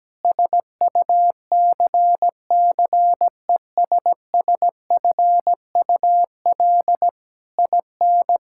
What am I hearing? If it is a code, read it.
Morse code, heard as a keyed tone, about 17 wpm, 700 Hz. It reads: SUCCESSFUL IN
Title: Sentences from Top 1000 Words - Review Part 6 - 17wpm